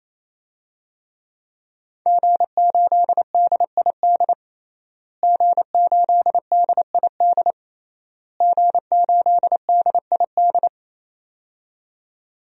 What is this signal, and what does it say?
Morse code, heard as a keyed tone, about 28 wpm, 700 Hz. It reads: G8DID G8DID G8DID
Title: DXCC-100 Call Signs - 5 Characters - Repeated 3x - 28wpm